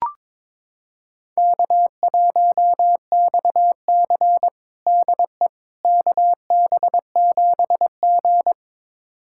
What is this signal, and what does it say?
Morse code, heard as a keyed tone, about 22 wpm, 700 Hz. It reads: K1XC DE KB7G
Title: USA Callsign DE Callsign - 4 Characters - 22wpm